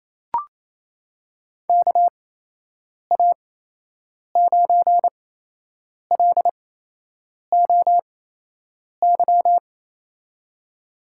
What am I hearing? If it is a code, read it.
Morse code, heard as a keyed tone, about 28 wpm, spaced as for 8 wpm, 700 Hz. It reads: KA9LOY
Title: USA Call Signs - 6 Characters - 8x Character Spacing - 28wpm